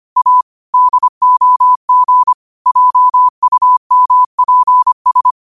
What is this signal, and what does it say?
Morse code, heard as a keyed tone, about 25 wpm, 1000 Hz. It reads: A DOG JUMPS